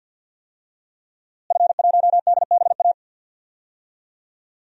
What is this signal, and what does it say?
Morse code, heard as a keyed tone, about 50 wpm, 700 Hz. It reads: F1DBA
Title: DXCC-100 Call Signs - 5 Characters - Repeated 3x - 50wpm